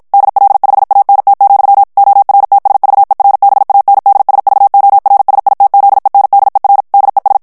Morse code, heard as a keyed tone, about 54 wpm, 800 Hz. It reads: DL5ANT, ORTSVERBANDSVORSITZENDER DES
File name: morse mp3 / ALLEMAND texte clair 2 vitesse 50 mots minute